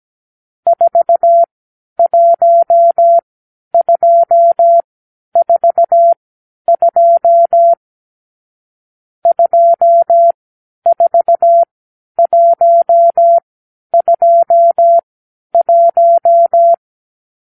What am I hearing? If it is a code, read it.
Morse code, heard as a keyed tone, about 17 wpm, spaced as for 10 wpm, 680 Hz. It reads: 41242 24121
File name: k7qo / Lesson 46 (1-4)